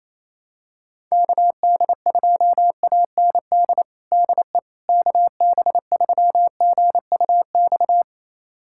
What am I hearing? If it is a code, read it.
Morse code, heard as a keyed tone, about 27 wpm, 700 Hz. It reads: KD2AND DE KB3GUX
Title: USA Callsign DE Callsign - 6 Characters - 28wpm